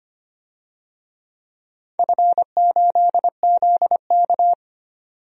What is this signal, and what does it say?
Morse code, heard as a keyed tone, about 25 wpm, 700 Hz. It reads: F8ZK
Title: DXCC-100 Call Signs - 4 Characters - 25wpm